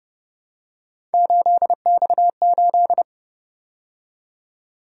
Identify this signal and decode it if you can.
Morse code, heard as a keyed tone, about 29 wpm, 700 Hz. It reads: 8X8